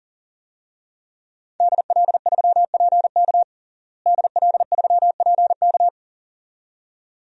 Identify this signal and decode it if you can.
Morse code, heard as a keyed tone, about 40 wpm, 700 Hz. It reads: DL3PK DL3PK
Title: DXCC-100 Call Signs - 5 Characters - Repeated 2x - 40wpm